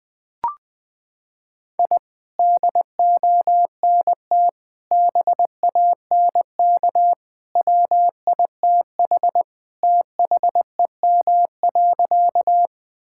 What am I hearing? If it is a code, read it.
Morse code, heard as a keyed tone, about 21 wpm, 700 Hz. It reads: I DONT BANK WITH THEM.
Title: Sentences from Top 1000 Words - Review Part 6 - 20wpm